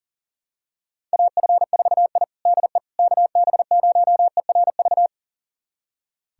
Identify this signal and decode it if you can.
Morse code, heard as a keyed tone, about 40 wpm, 700 Hz. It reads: AF4I DE KB0ERV
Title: USA Callsign DE Callsign 40wpm